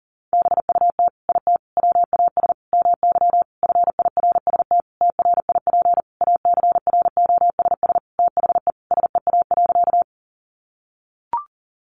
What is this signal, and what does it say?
Morse code, heard as a keyed tone, about 37 wpm, 700 Hz. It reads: BUT IT WAS MY FIRST TRIP ACROSS THE SEA.
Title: Sentences from Top 1300 Words 40wpm